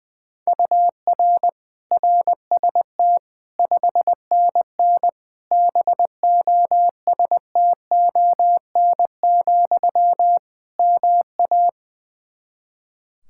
Morse code, heard as a keyed tone, about 21 wpm, 700 Hz. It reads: UR RST 5NN BOSTON, MA
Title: Practice QSO - Exchange 1 - Beginning - 20wpm